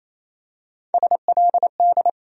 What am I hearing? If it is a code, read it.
Morse code, heard as a keyed tone, about 28 wpm, 700 Hz. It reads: SLD
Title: USA Call Sign Postfixes - 3 Letters 28wpm